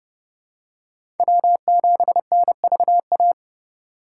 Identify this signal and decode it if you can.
Morse code, heard as a keyed tone, about 30 wpm, 700 Hz. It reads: W7NVA
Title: USA Call Signs - 5 Characters - 2x Character Spacing - 30wpm